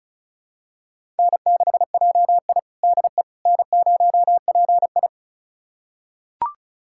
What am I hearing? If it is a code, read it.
Morse code, heard as a keyed tone, about 34 wpm, 700 Hz. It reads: N6JI DE N0PI
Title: USA Callsign DE Callsign - 4 Characters - 35wpm